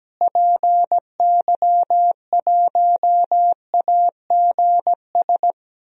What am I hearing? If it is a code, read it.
Morse code, heard as a keyed tone, about 17 wpm, 700 Hz. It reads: PY1AGS
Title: DXCC-100 Call Signs - 6 Characters - Repeated 2x - 17wpm